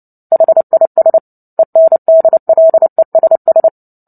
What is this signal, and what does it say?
Morse code, heard as a keyed tone, about 30 wpm, 650 Hz. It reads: HIS ENDLESS
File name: Book Two / 03-Chapter 3 - 29 WPM